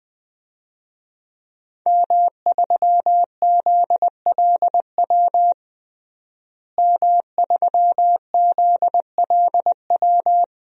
Morse code, heard as a keyed tone, about 20 wpm, 700 Hz. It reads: M3ZLW M3ZLW
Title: DXCC-100 Call Signs - 5 Characters - Repeated 2x - 20wpm